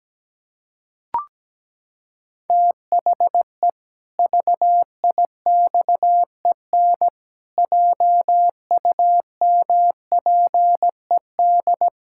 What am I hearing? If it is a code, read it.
Morse code, heard as a keyed tone, about 17 wpm, 700 Hz. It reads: THE VIXEN JUMPED
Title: Pangram Challenge v2 - 17wpm